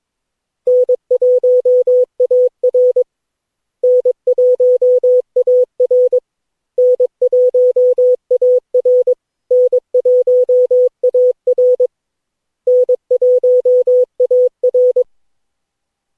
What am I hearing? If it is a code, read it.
Morse code, heard as a keyed tone, about 21 wpm, 500 Hz. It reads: N1AR N1AR N1AR N1AR N1AR
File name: cw / Session9 cw training